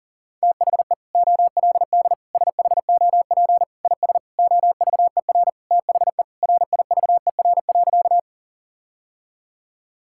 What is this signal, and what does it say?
Morse code, heard as a keyed tone, about 40 wpm, 700 Hz. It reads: THE OLD SHOP IS OVER THE RIVER.
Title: Sentences from Top 1600 Words 40wpm